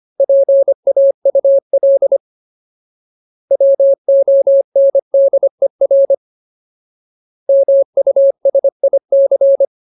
Morse code, heard as a keyed tone, about 25 wpm, 550 Hz. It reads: PAUL WONDER MUSIC
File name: Ultimate Word List / 25WPM Ultimate